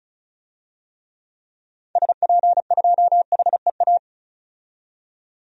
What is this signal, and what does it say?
Morse code, heard as a keyed tone, about 34 wpm, 700 Hz. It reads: SP2HEA
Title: DXCC-100 Call Signs - 6 Characters - 35wpm